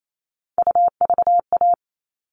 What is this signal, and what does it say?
Morse code, heard as a keyed tone, about 28 wpm, 700 Hz. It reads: UVA
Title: All Cut Numbers - 3-Digit 28wpm